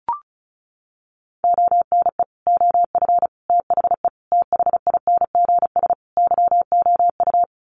Morse code, heard as a keyed tone, about 34 wpm, 700 Hz. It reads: ONE OF THE THINGS YOU
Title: Sets of 5 Words Encore2 35wpm